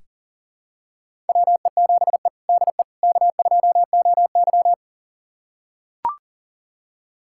Morse code, heard as a keyed tone, about 40 wpm, 700 Hz. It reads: WE7E DE K2OY
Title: USA Callsign DE Callsign - 4 Characters - 40wpm